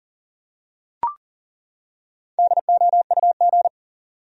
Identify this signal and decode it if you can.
Morse code, heard as a keyed tone, about 40 wpm, 700 Hz. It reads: DOUG
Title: USA Common Names - 40wpm